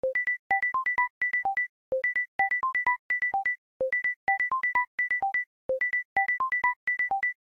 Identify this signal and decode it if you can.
There is no code.